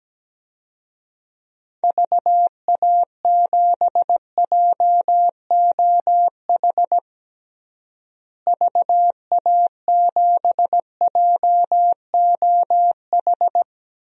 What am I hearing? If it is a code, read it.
Morse code, heard as a keyed tone, about 17 wpm, 700 Hz. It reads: VA7JOH VA7JOH
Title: DXCC-100 Call Signs - 6 Characters - Repeated 2x - 17wpm